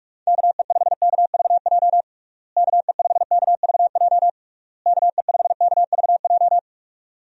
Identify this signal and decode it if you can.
Morse code, heard as a keyed tone, about 46 wpm, 700 Hz. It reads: KE5KVJ KE5KVJ KE5KVJ
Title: USA Call Signs - 6 Characters - Repeated 3x - 45wpm